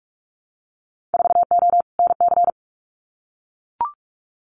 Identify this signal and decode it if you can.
Morse code, heard as a keyed tone, about 44 wpm, 700 Hz. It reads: 4O NC
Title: Winter Field Day Exchanges 45wpm